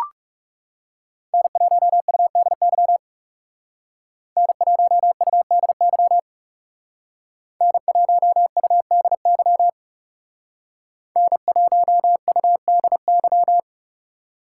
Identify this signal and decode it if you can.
Morse code, heard as a keyed tone, about 35 wpm, 700 Hz. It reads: N1UDY N1UDY N1UDY N1UDY
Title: Speed Racing - USA 1x3 Call Signs 30wpm